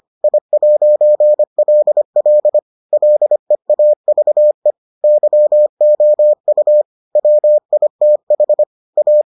Morse code, heard as a keyed tone, about 25 wpm, 600 Hz. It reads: I'LL LEAVE YOU WITH A